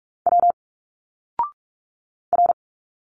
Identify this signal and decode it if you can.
Morse code, heard as a keyed tone, about 46 wpm, 700 Hz. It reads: W R
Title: Single Letters 45wpm